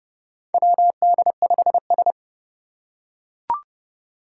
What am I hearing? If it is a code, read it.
Morse code, heard as a keyed tone, about 30 wpm, 700 Hz. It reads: WD5S
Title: Four Challenging Letter-Number Combinations 30wpm